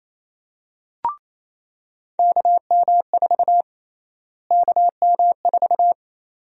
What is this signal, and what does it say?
Morse code, heard as a keyed tone, about 28 wpm, 700 Hz. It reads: KM4 KM4
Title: USA Call Sign Prefixes - Repeated 2x - 28wpm